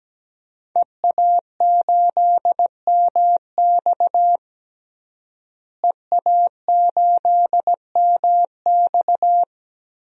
Morse code, heard as a keyed tone, about 17 wpm, 700 Hz. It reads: EA8MX EA8MX
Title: DXCC-100 Call Signs - 5 Characters - Repeated 3x - 17wpm